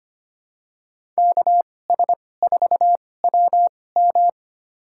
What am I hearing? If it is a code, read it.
Morse code, heard as a keyed tone, about 25 wpm, spaced as for 18 wpm, 700 Hz. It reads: KS4WM
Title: USA Call Signs - 2x Character Spacing - 25wpm